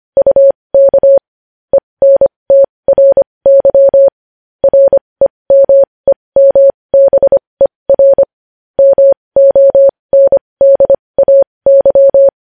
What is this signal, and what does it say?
Morse code, heard as a keyed tone, about 25 wpm, spaced as for 20 wpm, 550 Hz. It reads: UK ENTRY REMEMBER MONDAY
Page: news-headlines-in-morse-code-at-20-wpm